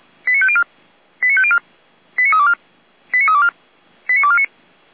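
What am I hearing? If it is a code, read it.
Five-tone selective call (ZVEI), heard as quick runs of five tones, tones of about 70 ms. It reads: 89504 89504 89324 89324 89249
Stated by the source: ZVEI five-tone selective call received on an SDR, recorded by fire brigade Feuerwehr Orlamünde.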